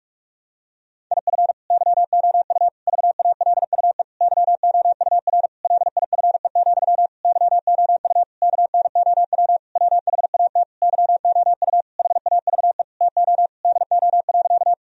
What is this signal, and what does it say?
Morse code, heard as a keyed tone, about 46 wpm, 700 Hz. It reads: IF YOU VALUE YOUR LIFE, YOU KNOW WHAT YOU HAVE TO DO.
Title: Sentences from Top 1000 Words - Review Part 3 - 45wpm